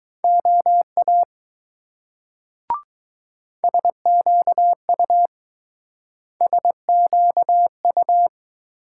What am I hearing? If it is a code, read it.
Morse code, heard as a keyed tone, about 19 wpm, 700 Hz. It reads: OA SQU SQU